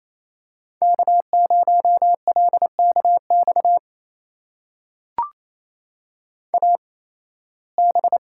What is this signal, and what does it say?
Morse code, heard as a keyed tone, about 28 wpm, spaced as for 18 wpm, 700 Hz. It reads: K0LKX AB
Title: USA Call Signs - 5 Characters - 8x Character Spacing - 28wpm